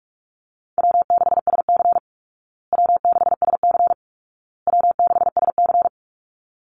Morse code, heard as a keyed tone, about 44 wpm, 700 Hz. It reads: W6SC W6SC W6SC
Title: Call Signs Repeated 2x - 45wpm